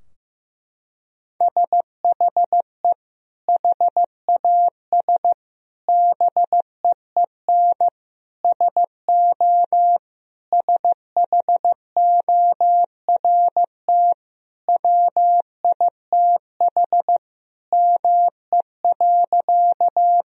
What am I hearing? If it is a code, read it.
Morse code, heard as a keyed tone, about 16 wpm, 700 Hz. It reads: SHE HAS BEEN SO SHORT WITH ME.
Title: Sentences from Top 1000 Words - Review Part 2 - 15wpm